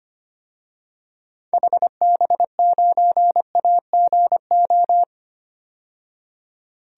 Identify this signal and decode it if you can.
Morse code, heard as a keyed tone, about 25 wpm, 700 Hz. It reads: HB9AGO
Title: DXCC-100 Call Signs - 6 Characters - 25wpm